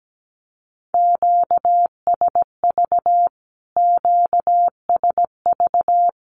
Morse code, heard as a keyed tone, about 17 wpm, 700 Hz. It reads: QSV QSV
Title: Q-Codes Repeated 3x 17wpm